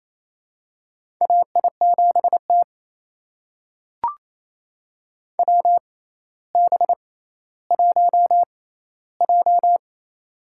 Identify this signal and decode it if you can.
Morse code, heard as a keyed tone, about 28 wpm, spaced as for 14 wpm, 700 Hz. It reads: AI7T WB1J
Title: USA Call Signs - 4 Characters - 6x Character Spacing - 28wpm